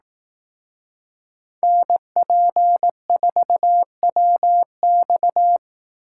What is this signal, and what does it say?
Morse code, heard as a keyed tone, about 18 wpm, 700 Hz. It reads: NP4WX
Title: Speed Racing - USA 2x2 Call Signs 15wpm